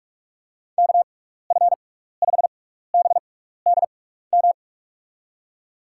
Morse code, heard as a keyed tone, about 46 wpm, spaced as for 15 wpm, 700 Hz. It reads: KF5BDM
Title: USA Call Signs - 6 Characters - 6x Character Spacing - 45wpm